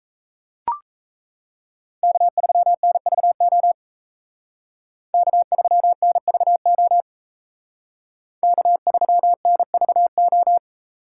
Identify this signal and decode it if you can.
Morse code, heard as a keyed tone, about 36 wpm, 700 Hz. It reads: K3NVO K3NVO K3NVO